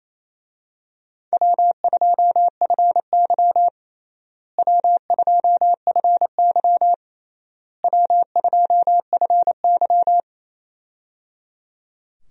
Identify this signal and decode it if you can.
Morse code, heard as a keyed tone, about 28 wpm, 700 Hz. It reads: W2FY W2FY W2FY
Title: USA Call Signs - 4 Characters - Repeated 3x - 28wpm